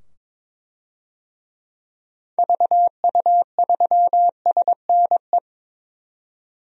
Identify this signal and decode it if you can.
Morse code, heard as a keyed tone, about 22 wpm, 700 Hz. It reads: VU3SNE
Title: DXCC-100 Call Signs - 6 Characters - 22wpm